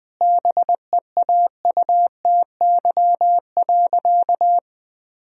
Morse code, heard as a keyed tone, about 21 wpm, 700 Hz. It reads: BEAUTY.
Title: Sentences from Top 1000 Words - Review Part 2 - 20wpm